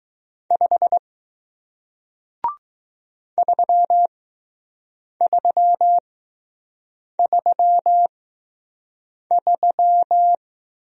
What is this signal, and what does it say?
Morse code, heard as a keyed tone, about 20 wpm, spaced as for 6 wpm, 700 Hz. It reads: 5 3333